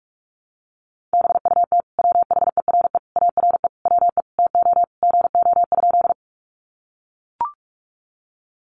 Morse code, heard as a keyed tone, about 44 wpm, 700 Hz. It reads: BUT WHERE ARE WE TO GO?
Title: Sentences from Top 200 Words 45wpm